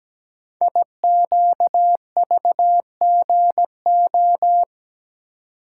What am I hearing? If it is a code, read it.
Morse code, heard as a keyed tone, about 17 wpm, 700 Hz. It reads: IQVGO
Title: Code Groups - 5 Letters - 17wpm